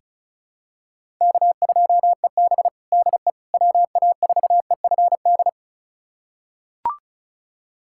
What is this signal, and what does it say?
Morse code, heard as a keyed tone, about 34 wpm, 700 Hz. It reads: K2EB DE WA4EFD
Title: USA Callsign DE Callsign 35wpm